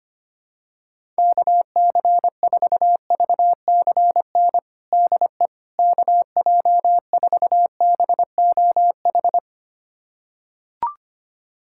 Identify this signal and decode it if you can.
Morse code, heard as a keyed tone, about 25 wpm, 700 Hz. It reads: KC4VCN DE KJ4BOH
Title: USA Callsign DE Callsign - 6 Characters - 25wpm